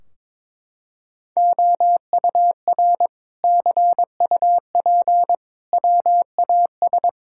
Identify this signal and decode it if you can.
Morse code, heard as a keyed tone, about 22 wpm, 700 Hz. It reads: OUR CUP WAS
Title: Sentences from Top 1600 Words 22wpm